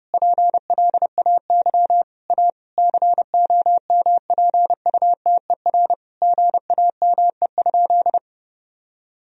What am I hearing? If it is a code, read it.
Morse code, heard as a keyed tone, about 30 wpm, 700 Hz. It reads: PLAY A COMPUTER GAME?
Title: Sentences from Top 1000 Words - Review Part 3 - 30wpm